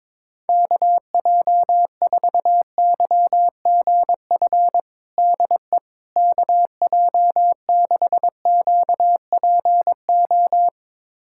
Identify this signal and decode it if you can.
Morse code, heard as a keyed tone, about 22 wpm, 700 Hz. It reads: KJ4YGF DE KJ6QPO
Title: USA Callsign DE Callsign 22wpm